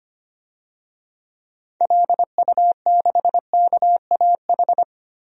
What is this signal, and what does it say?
Morse code, heard as a keyed tone, about 25 wpm, 700 Hz. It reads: LU6KAH
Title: DXCC-100 Call Signs - 6 Characters - Repeated 2x - 25wpm